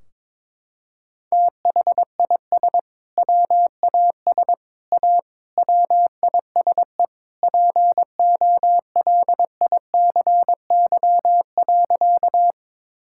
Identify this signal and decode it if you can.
Morse code, heard as a keyed tone, about 22 wpm, 700 Hz. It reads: THIS WAS A WISE POLICY.
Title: Sentences from Top 1600 Words 22wpm